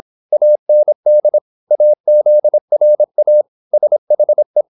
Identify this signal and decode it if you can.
Morse code, heard as a keyed tone, about 26 wpm, 600 Hz. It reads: AND AZRA SHE
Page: Never 26wpm | CW med Gnister